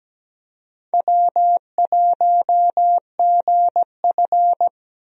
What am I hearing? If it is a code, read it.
Morse code, heard as a keyed tone, about 17 wpm, 700 Hz. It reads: W1GF
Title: USA Call Signs - 4 Characters - 17wpm